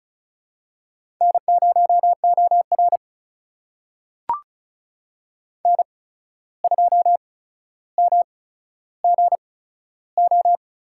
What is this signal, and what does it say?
Morse code, heard as a keyed tone, about 36 wpm, spaced as for 14 wpm, 700 Hz. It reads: N0OR N2MGO
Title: USA Call Signs - 8x Character Spacing - 35wpm